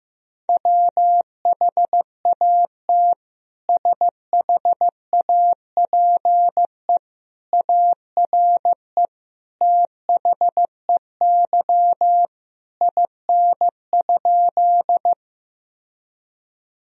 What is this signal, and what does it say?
Morse code, heard as a keyed tone, about 15 wpm, 700 Hz. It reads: WHAT SHAPE ARE THEY IN?
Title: Sentences from Top 1000 Words - Review Part 3 - 15wpm